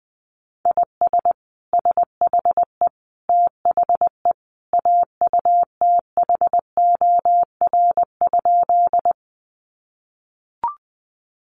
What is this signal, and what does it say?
Morse code, heard as a keyed tone, about 19 wpm, 700 Hz. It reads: IS SHE THE AUTHOR?
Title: Sentences from Top 1200 Words 20wpm